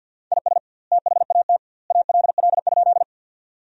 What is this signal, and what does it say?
Morse code, heard as a keyed tone, about 50 wpm, 700 Hz. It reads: IS THAT ALL?
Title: 2x Word-Spacing - Sentences from Top 100 Words - 50wpm